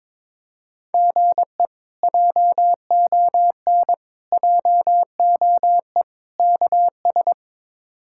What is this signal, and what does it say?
Morse code, heard as a keyed tone, about 22 wpm, 700 Hz. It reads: GE JON JOE KS
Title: K1USN SST QSO Practice - 22wpm